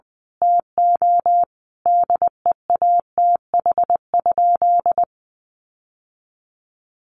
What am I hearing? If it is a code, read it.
Morse code, heard as a keyed tone, about 19 wpm, 700 Hz. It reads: TO DEATH?
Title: Sentences from Top 1100 Words 20wpm